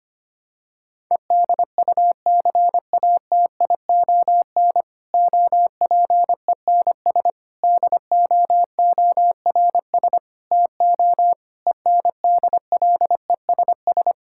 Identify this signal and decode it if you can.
Morse code, heard as a keyed tone, about 25 wpm, 700 Hz. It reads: EDUCATION OPENS DOORS TO ENDLESS
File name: practice / 25wpm / Difficult Sentences with Hints - 25wpm